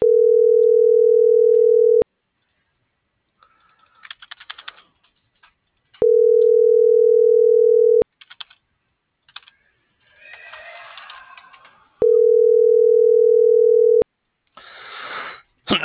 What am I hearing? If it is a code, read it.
Ringback tone (North America), 440 + 480 Hz.